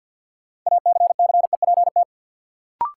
Morse code, heard as a keyed tone, about 53 wpm, 700 Hz. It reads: ACCEPT
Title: Sentences from Top 1300 Words - New Word Drill Alt - 50wpm